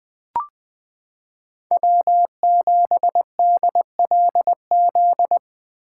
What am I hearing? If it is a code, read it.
Morse code, heard as a keyed tone, about 20 wpm, 700 Hz. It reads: W7DLZ
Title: 2023 POTA Activator Call Signs - 20wpm